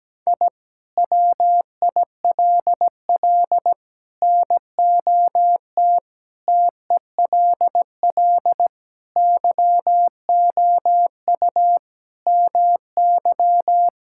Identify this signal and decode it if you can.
Morse code, heard as a keyed tone, about 17 wpm, 700 Hz. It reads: I WILL NOT TELL YOU MY
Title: Sentences from Top 1000 Words - Review Part 5 - 17wpm